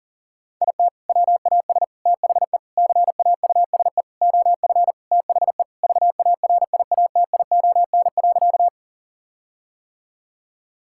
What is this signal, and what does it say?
Morse code, heard as a keyed tone, about 40 wpm, 700 Hz. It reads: IT WAS THE CAUSE OF THE VARIATION.
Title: Sentences from Top 1600 Words 40wpm